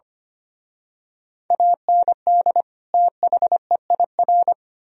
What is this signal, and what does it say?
Morse code, heard as a keyed tone, about 29 wpm, 700 Hz. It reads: AND THEIR
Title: Speed-Racing - Sets of 2 Words 25wpm